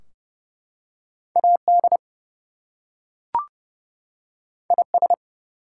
Morse code, heard as a keyed tone, about 29 wpm, 700 Hz. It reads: AD IS
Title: Two Letter Words - 30wpm